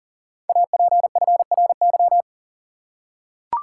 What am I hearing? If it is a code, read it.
Morse code, heard as a keyed tone, about 40 wpm, 700 Hz. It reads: APFRY
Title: Code Groups - 5 Letters - 40wpm